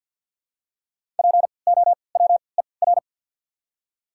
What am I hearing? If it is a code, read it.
Morse code, heard as a keyed tone, about 52 wpm, spaced as for 29 wpm, 700 Hz. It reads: POWER
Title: 3x Character Spacing - Top 500 Words - 50wpm